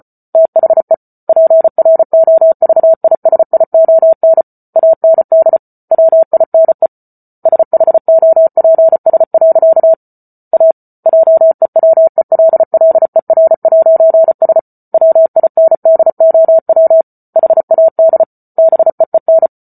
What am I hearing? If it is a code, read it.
Morse code, heard as a keyed tone, about 34 wpm, 650 Hz. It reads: THE PROVISION AND WINE SHOPS. A JEWELLER'S WINDOW HAD BEEN